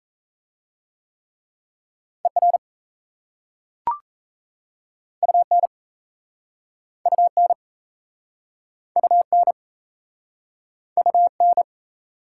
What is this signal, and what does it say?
Morse code, heard as a keyed tone, about 38 wpm, 700 Hz. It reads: ER UN UN UN UN